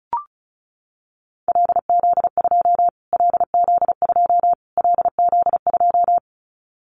Morse code, heard as a keyed tone, about 34 wpm, 700 Hz. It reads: LZ2 LZ2 LZ2
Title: Callsign Prefixes for Top 25 DXCC - Repeated 3x - 35wpm